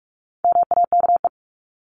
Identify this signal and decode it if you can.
Morse code, heard as a keyed tone, about 44 wpm, 700 Hz. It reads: MAKE
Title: Top 500 Words 45wpm